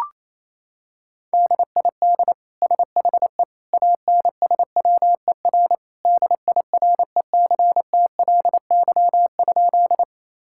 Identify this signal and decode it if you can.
Morse code, heard as a keyed tone, about 28 wpm, 700 Hz. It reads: DID SHE ANSWER DIRECTLY?
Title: Sentences from Top 1000 Words - Review Part 5 - 28wpm